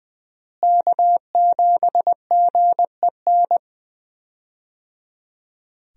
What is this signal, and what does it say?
Morse code, heard as a keyed tone, about 20 wpm, 700 Hz. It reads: K7GEN
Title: USA Call Signs - 5 Characters - Repeated 3x - 20wpm